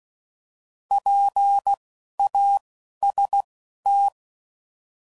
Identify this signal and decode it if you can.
Morse code, heard as a keyed tone, about 16 wpm, spaced as for 11 wpm, 800 Hz. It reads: PAST
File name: Prefixes / By First Letter / Ps